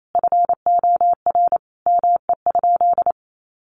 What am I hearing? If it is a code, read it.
Morse code, heard as a keyed tone, about 28 wpm, 700 Hz. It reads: FOR ME?
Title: Sentences from Top 1100 Words 28wpm